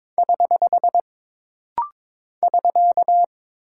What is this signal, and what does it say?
Morse code, heard as a keyed tone, about 22 wpm, 700 Hz. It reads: <HH> <SK>
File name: practice / 22wpm / Single Punctuation-Prosign - Rapid-Fire - 22wpm